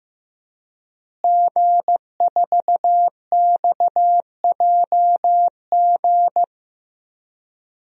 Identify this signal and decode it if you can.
Morse code, heard as a keyed tone, about 15 wpm, 700 Hz. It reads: G4XJG
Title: DXCC-100 Call Signs - 5 Characters - Repeated 3x - 15wpm